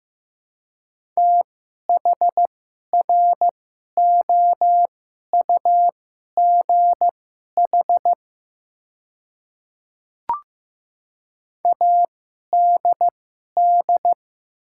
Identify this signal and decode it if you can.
Morse code, heard as a keyed tone, about 15 wpm, spaced as for 11 wpm, 700 Hz. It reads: THROUGH ADD
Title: 2x Character Spacing - Top 500 Words - 15wpm